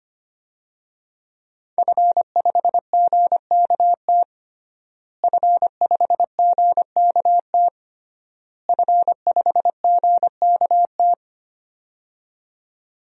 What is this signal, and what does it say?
Morse code, heard as a keyed tone, about 25 wpm, 700 Hz. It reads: F5GKT F5GKT F5GKT
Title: DXCC-100 Call Signs - 5 Characters - Repeated 3x - 25wpm